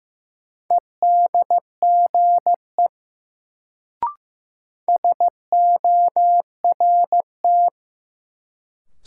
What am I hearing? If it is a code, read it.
Morse code, heard as a keyed tone, about 15 wpm, 700 Hz. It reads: EDGE SORT
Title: Sentences from Top 1300 Words - New Word Drill Alt - 15wpm